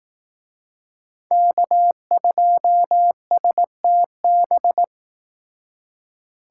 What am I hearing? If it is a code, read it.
Morse code, heard as a keyed tone, about 19 wpm, 700 Hz. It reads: K2STB
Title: Speed Racing - USA 1x3 Call Signs 15wpm